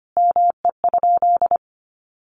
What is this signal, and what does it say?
Morse code, heard as a keyed tone, about 25 wpm, 700 Hz. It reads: ME?
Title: Sentences from Top 1200 Words 25wpm